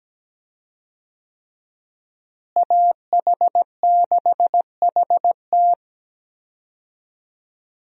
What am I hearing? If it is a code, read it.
Morse code, heard as a keyed tone, about 17 wpm, 700 Hz. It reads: AH6HT
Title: USA Call Signs - 5 Characters - Repeated 3x - 17wpm